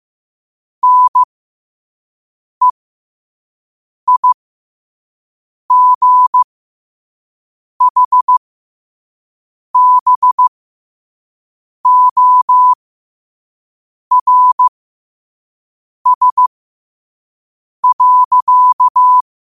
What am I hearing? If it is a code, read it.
Morse code, heard as a keyed tone, about 15 wpm, spaced as for 5 wpm, 1000 Hz. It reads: NEIGHBORS.
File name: qotdfiles / 5wpm